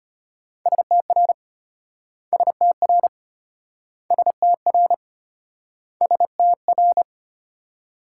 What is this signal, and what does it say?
Morse code, heard as a keyed tone, about 32 wpm, 700 Hz. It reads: STR STR STR STR